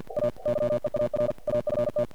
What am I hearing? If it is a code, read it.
Morse code, heard as a keyed tone, about 48 wpm, 590 Hz. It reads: A YEAR AFT
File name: archived / CRSnet